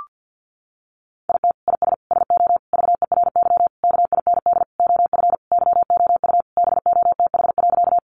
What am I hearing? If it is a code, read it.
Morse code, heard as a keyed tone, about 50 wpm, 700 Hz. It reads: IT IS SO VERY KIND OF YOU BOTH.
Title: Sentences from Top 300 Words 50wpm